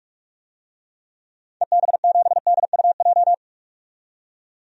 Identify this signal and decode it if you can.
Morse code, heard as a keyed tone, about 42 wpm, 700 Hz. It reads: EB7DUJ
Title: DXCC-100 Call Signs - 6 Characters - 45wpm